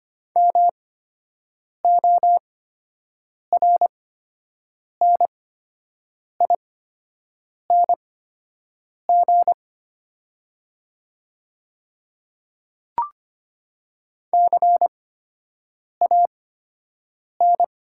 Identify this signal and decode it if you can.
Morse code, heard as a keyed tone, about 25 wpm, spaced as for 7 wpm, 700 Hz. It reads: MORNING CAN
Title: 8x Character Spacing - Top 500 Words - 25wpm